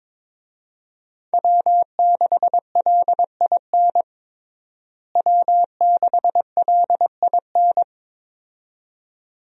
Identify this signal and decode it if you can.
Morse code, heard as a keyed tone, about 21 wpm, 700 Hz. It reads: W6LIN W6LIN
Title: USA Call Signs - 5 Characters - Repeated 2x - 22wpm